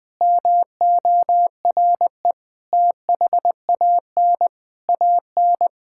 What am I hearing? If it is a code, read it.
Morse code, heard as a keyed tone, about 20 wpm, 700 Hz. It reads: MORE THAN AN
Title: Sentences with 3-Word Phrases - 20wpm